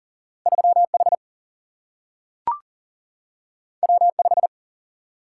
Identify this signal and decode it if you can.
Morse code, heard as a keyed tone, about 42 wpm, 700 Hz. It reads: 3H W5
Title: Single-Letter Single-Number 40wpm